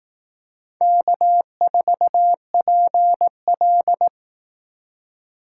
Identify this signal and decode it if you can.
Morse code, heard as a keyed tone, about 18 wpm, 700 Hz. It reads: K4PL
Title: Speed Racing - USA Call Signs 15wpm